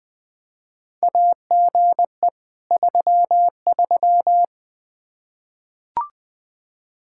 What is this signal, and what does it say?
Morse code, heard as a keyed tone, about 20 wpm, 700 Hz. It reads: AGE 33
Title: Practice QSO - Exchange 3 - More Info - 20wpm